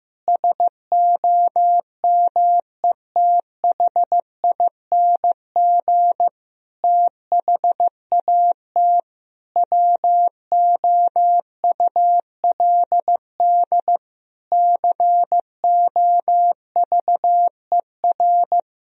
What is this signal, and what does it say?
Morse code, heard as a keyed tone, about 15 wpm, 700 Hz. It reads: SOMETHING THAT WOULD COVER
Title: Sentences from Top 1000 Words - Review Part 1 - 15wpm